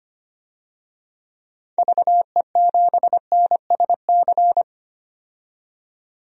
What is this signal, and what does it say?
Morse code, heard as a keyed tone, about 25 wpm, 700 Hz. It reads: VE7NSC
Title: DXCC-100 Call Signs - 6 Characters - 25wpm